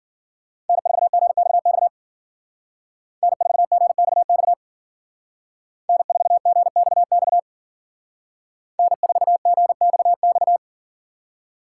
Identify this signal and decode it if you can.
Morse code, heard as a keyed tone, about 52 wpm, 700 Hz. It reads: N4GXX N4GXX N4GXX N4GXX